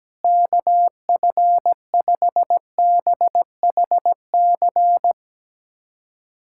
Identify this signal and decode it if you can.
Morse code, heard as a keyed tone, about 17 wpm, 700 Hz. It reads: KF5BHC
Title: USA Callsign DE Callsign 17wpm